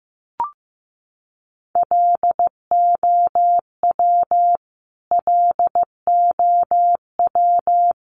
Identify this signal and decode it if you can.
Morse code, heard as a keyed tone, about 15 wpm, 700 Hz. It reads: LOW LOW
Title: Top 200 Words Repeated 15wpm